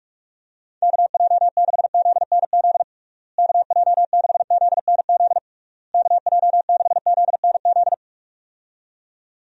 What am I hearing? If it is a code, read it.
Morse code, heard as a keyed tone, about 48 wpm, 700 Hz. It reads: KJ6ZNZ KJ6ZNZ KJ6ZNZ
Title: USA Call Signs - 6 Characters - Repeated 3x - 45wpm